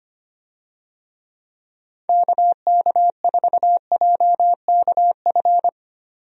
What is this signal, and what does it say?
Morse code, heard as a keyed tone, about 25 wpm, 700 Hz. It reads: KK4JKF
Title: USA Call Signs - 6 Characters - 25wpm